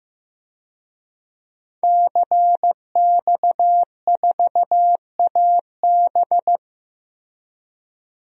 Morse code, heard as a keyed tone, about 15 wpm, 700 Hz. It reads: CX4AB
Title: DXCC-100 Call Signs - 5 Characters - Repeated 3x - 15wpm